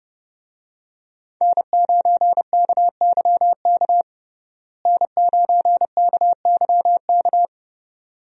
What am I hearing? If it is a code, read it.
Morse code, heard as a keyed tone, about 30 wpm, 700 Hz. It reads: N9KYK N9KYK
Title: USA Call Signs - 5 Characters - Repeated 2x - 30wpm